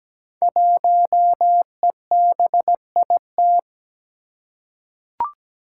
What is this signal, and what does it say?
Morse code, heard as a keyed tone, about 17 wpm, 700 Hz. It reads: 1EBIT
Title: Five Challenging Letter-Number Combinations 17wpm